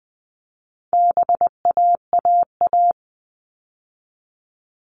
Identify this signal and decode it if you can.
Morse code, heard as a keyed tone, about 20 wpm, 700 Hz. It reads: BAAA
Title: All Cut Numbers - 4-Digit 20wpm